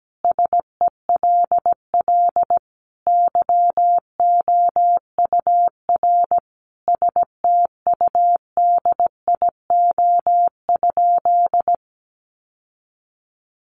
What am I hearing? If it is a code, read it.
Morse code, heard as a keyed tone, about 17 wpm, 700 Hz. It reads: SELL YOUR STUDIO?
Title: Sentences from Top 1200 Words 17wpm